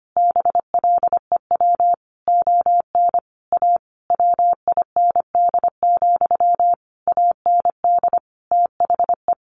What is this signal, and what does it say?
Morse code, heard as a keyed tone, about 25 wpm, 700 Hz. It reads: BLEW ON A WIND, AND THE
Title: A Sound Of Thunder 25wpm